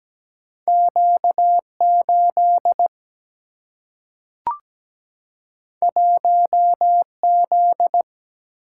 Morse code, heard as a keyed tone, about 17 wpm, 700 Hz. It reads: Q8 1Z
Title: Single-Letter Single-Number 17wpm